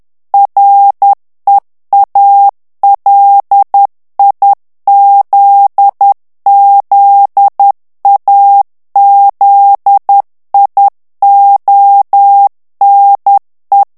Morse code, heard as a keyed tone, about 10 wpm, 800 Hz. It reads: REALIZZAZIONE
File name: morse mp3 / ITALIEN texte clair 1 vitesse 10 mots minute